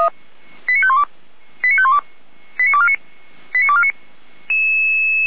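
Five-tone selective call (ZVEI), heard as quick runs of five tones, tones of about 70 ms. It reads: 89512 89512 89249 89249
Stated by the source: ZVEI five-tone call 89558 received by SDR.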